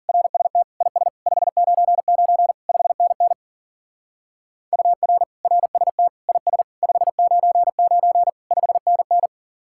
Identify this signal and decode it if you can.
Morse code, heard as a keyed tone, about 48 wpm, 700 Hz. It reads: RST IS 599 5NN UR RST IS 599 5NN
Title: Speed-Racing Common QSO Phrases - 40wpm